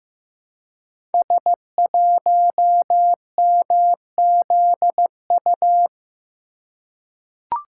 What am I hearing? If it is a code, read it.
Morse code, heard as a keyed tone, about 15 wpm, 700 Hz. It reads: S1MZU
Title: Code Groups - 5 Letter:Numbers - 15wpm